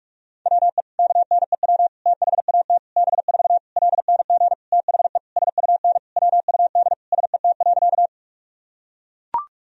Morse code, heard as a keyed tone, about 43 wpm, 700 Hz. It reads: WE KNEW THAT B4 LNG THE SUN WUD SET.
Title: Sentences from Top 400 Words and Abbreviations 45wpm